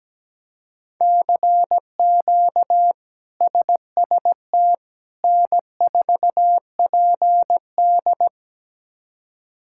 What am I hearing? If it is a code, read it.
Morse code, heard as a keyed tone, about 17 wpm, 700 Hz. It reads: CQ SST N4PD
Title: K1USN SST QSO Practice - 17wpm